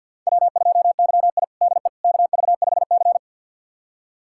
Morse code, heard as a keyed tone, about 52 wpm, 700 Hz. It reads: W2YI DE KV5C
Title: USA Callsign DE Callsign - 4 Characters - 50wpm